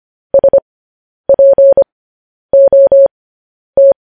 Morse code, heard as a keyed tone, about 25 wpm, spaced as for 10 wpm, 550 Hz. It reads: SPOT